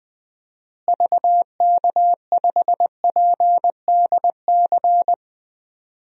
Morse code, heard as a keyed tone, about 20 wpm, 700 Hz. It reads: VK5PDC
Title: DXCC-100 Call Signs - 6 Characters - Repeated 2x - 20wpm